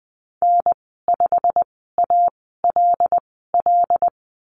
Morse code, heard as a keyed tone, about 20 wpm, spaced as for 15 wpm, 700 Hz. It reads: N5ALL
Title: 122 Lesson 20g - Number 3 - Call Signs 20wpm